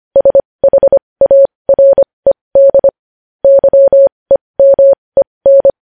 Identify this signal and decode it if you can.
Morse code, heard as a keyed tone, about 25 wpm, spaced as for 20 wpm, 550 Hz. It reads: SHARED YEMEN
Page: News Headlines in Morse Code at 20 WPM